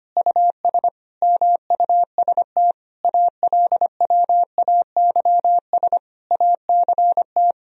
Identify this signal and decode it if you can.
Morse code, heard as a keyed tone, about 25 wpm, 700 Hz. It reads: US MUST ALWAYS ACT
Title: Sentences from Top 1000 Words - Review Part 1 - 25wpm